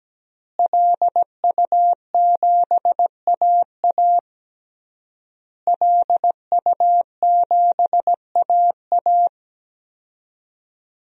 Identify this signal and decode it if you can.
Morse code, heard as a keyed tone, about 17 wpm, 700 Hz. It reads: LU7AA LU7AA
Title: DXCC-100 Call Signs - 5 Characters - Repeated 3x - 17wpm